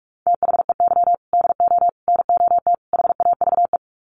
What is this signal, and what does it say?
Morse code, heard as a keyed tone, about 44 wpm, 700 Hz. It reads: THEY DO NOT HAVE
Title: Sets of 4 Words 45wpm